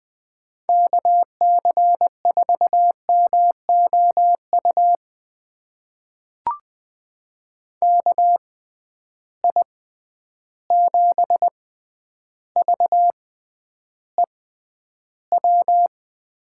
Morse code, heard as a keyed tone, about 20 wpm, spaced as for 10 wpm, 700 Hz. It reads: KC4MOU KI7VEW
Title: USA Call Signs - 6 Characters - 6x Character Spacing - 20wpm